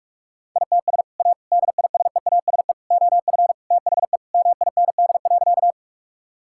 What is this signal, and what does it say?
Morse code, heard as a keyed tone, about 41 wpm, 700 Hz. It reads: ITS A DISEASE OF THE MIND.
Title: Sentences from Top 1000 Words - Review Part 6 - 45wpm